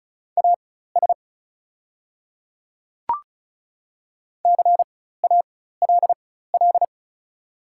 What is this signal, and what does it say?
Morse code, heard as a keyed tone, about 34 wpm, spaced as for 16 wpm, 690 Hz. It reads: AS CALL
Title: 4x Character Spacing - Top 300 Words - 35wpm